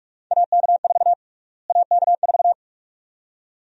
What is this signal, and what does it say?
Morse code, heard as a keyed tone, about 43 wpm, 700 Hz. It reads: AK4 AK4
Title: USA Call Sign Prefixes - Repeated 2x - 45wpm